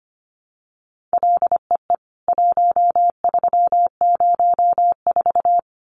Morse code, heard as a keyed tone, about 25 wpm, 700 Hz. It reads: LEE 1304
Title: CWT Exchange 25wpm